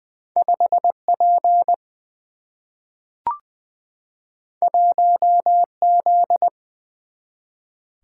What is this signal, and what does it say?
Morse code, heard as a keyed tone, about 20 wpm, 700 Hz. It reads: 5P 1Z
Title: Single-Letter Single-Number 20wpm